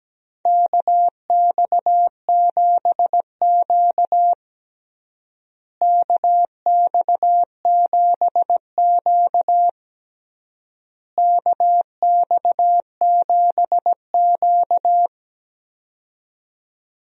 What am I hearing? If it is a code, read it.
Morse code, heard as a keyed tone, about 17 wpm, 700 Hz. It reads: KX7Q KX7Q KX7Q
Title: USA Call Signs - 4 Characters - Repeated 3x - 17wpm